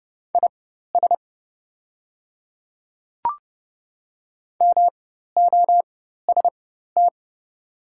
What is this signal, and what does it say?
Morse code, heard as a keyed tone, about 30 wpm, spaced as for 14 wpm, 690 Hz. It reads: IS MOST
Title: 4x Character Spacing - Top 100 Words - 30wpm